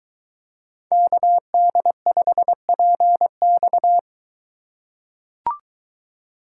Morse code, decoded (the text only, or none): KD5PX